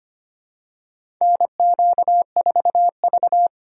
Morse code, heard as a keyed tone, about 25 wpm, 700 Hz. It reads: NQ4V
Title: USA Call Signs - 4 Characters - 4x Character Spacing - 25wpm